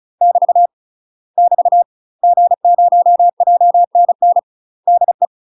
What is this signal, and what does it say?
Morse code, heard as a keyed tone, about 36 wpm, 700 Hz. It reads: <BT> <BT> G0JNN DE
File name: UK MASTER QSOs TRAINING / 35WPM / UK MASTER QSOs 0044